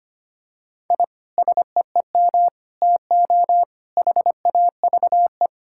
Morse code, heard as a keyed tone, about 25 wpm, 700 Hz. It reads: I SEEM TO HAVE
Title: Sentences from Top 1000 Words - Review Part 5 - 25wpm